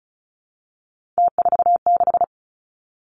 Morse code, heard as a keyed tone, about 36 wpm, 700 Hz. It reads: T46
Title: Three Challenging Letter-Number Combinations 35wpm